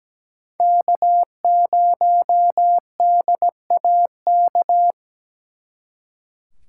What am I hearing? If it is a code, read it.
Morse code, heard as a keyed tone, about 20 wpm, 700 Hz. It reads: K0DAK
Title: Speed Racing - USA 1x3 Call Signs 17wpm